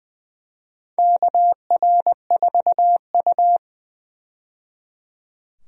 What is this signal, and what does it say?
Morse code, heard as a keyed tone, about 20 wpm, 700 Hz. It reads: KR4U
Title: USA Call Signs - 4 Characters - 20wpm